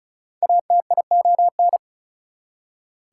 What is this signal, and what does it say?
Morse code, heard as a keyed tone, about 36 wpm, 700 Hz. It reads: ATION
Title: Prefixes and Suffixes 35wpm